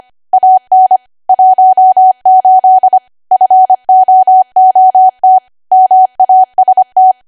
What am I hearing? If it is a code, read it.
Morse code, heard as a keyed tone, about 25 wpm, 750 Hz. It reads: AN 18 FOOT MAST
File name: Archive / 25 WPM / 25WPM